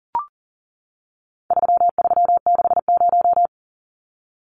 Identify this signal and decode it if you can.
Morse code, heard as a keyed tone, about 40 wpm, 700 Hz. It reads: 3360
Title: Numbers - 4-Digit 40wpm